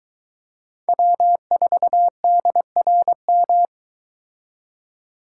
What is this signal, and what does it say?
Morse code, heard as a keyed tone, about 23 wpm, 700 Hz. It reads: W4DRM